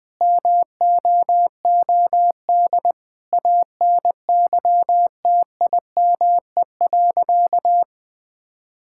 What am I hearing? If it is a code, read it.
Morse code, heard as a keyed tone, about 21 wpm, 700 Hz. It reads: MOOD ANYTIME.
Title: Difficult Sentences with Hints - 20wpm